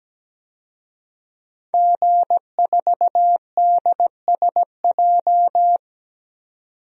Morse code, heard as a keyed tone, about 17 wpm, 700 Hz. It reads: G4DSJ
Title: DXCC-100 Call Signs - 5 Characters - 17wpm